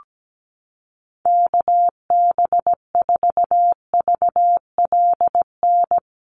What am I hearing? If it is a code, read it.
Morse code, heard as a keyed tone, about 17 wpm, 700 Hz. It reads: KB4VLN
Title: USA Call Signs Repeated 2x 17wpm